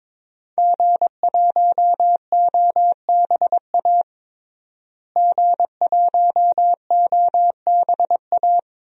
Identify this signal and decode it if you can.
Morse code, heard as a keyed tone, about 22 wpm, 700 Hz. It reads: G1OBA G1OBA
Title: DXCC-100 Call Signs - 5 Characters - Repeated 3x - 22wpm